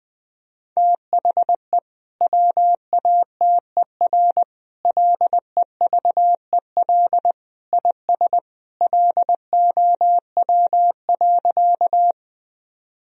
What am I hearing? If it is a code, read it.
Morse code, heard as a keyed tone, about 20 wpm, 700 Hz. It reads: THE WATER LEVEL IS LOW.
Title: Sentences from Top 1000 Words - Review Part 4 - 20wpm